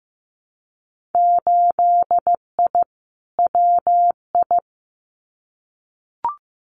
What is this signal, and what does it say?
Morse code, heard as a keyed tone, about 15 wpm, 700 Hz. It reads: 8I WI
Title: Winter Field Day Exchanges 15wpm